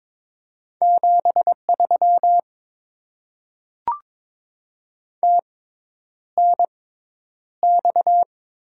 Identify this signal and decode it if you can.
Morse code, heard as a keyed tone, about 22 wpm, spaced as for 10 wpm, 700 Hz. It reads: 73 TNX
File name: Top 90 QSO Elements - 6x Character Spacing - 22wpm